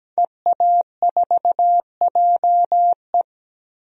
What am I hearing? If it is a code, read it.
Morse code, heard as a keyed tone, about 17 wpm, 700 Hz. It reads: EA4JE
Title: DXCC-100 Call Signs - 5 Characters - Repeated 2x - 17wpm